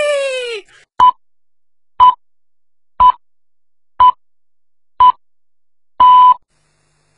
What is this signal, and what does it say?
Radio time signal (time pips). It is heard from 1.0 to 6.3 s.